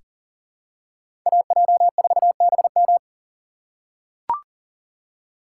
Morse code, heard as a keyed tone, about 37 wpm, 700 Hz. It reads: AJ4BM
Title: USA Call Signs - 5 Characters - 2x Character Spacing - 40wpm